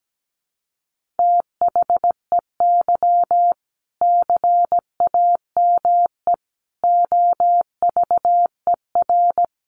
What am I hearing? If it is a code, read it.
Morse code, heard as a keyed tone, about 17 wpm, 700 Hz. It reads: THEY CAME OVER
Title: Sentences from Top 300 Words 17wpm